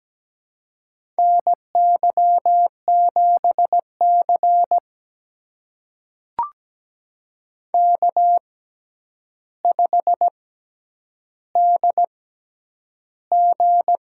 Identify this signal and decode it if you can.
Morse code, heard as a keyed tone, about 17 wpm, spaced as for 9 wpm, 700 Hz. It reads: NY7C K5DG
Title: USA Call Signs - 4 Characters - 6x Character Spacing - 17wpm